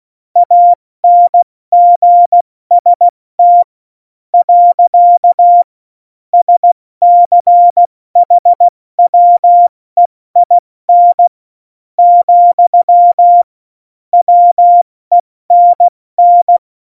Morse code, heard as a keyed tone, about 16 wpm, spaced as for 13 wpm, 700 Hz. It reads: ANGST. SCHWEIN, WENN
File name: Morsekurs / Witze / Witz 18. Dezember, 70 F80